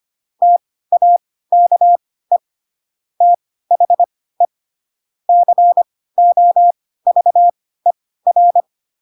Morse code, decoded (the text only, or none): TAKE THE COVER